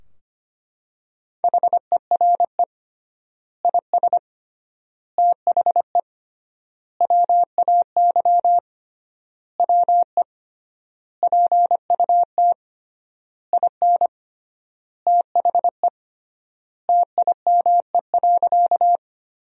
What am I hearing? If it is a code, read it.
Morse code, heard as a keyed tone, about 25 wpm, 700 Hz. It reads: HERE IS THE WAY WE PUT IN THE TIME.
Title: 3x Word-Spacing - Sentences from Top 200 Words - 25wpm